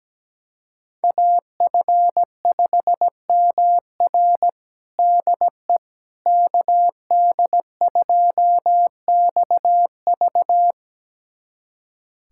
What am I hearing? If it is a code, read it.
Morse code, heard as a keyed tone, about 17 wpm, 700 Hz. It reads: AF5MR DE KD2XV
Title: USA Callsign DE Callsign - 5 Characters - 17wpm